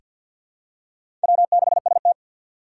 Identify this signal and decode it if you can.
Morse code, heard as a keyed tone, about 47 wpm, 700 Hz. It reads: W6ST
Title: USA Call Signs - 4 Characters - 3x Character Spacing - 50wpm